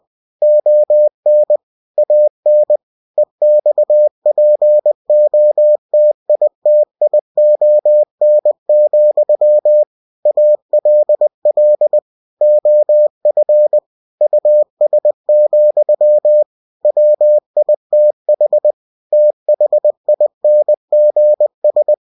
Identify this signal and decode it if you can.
Morse code, heard as a keyed tone, about 20 wpm, 600 Hz. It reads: ON AN EXPOTITION, ALL OF US, WITH THINGS
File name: cw-books / winnie-the-pooh / winnie-20wpm-page-0079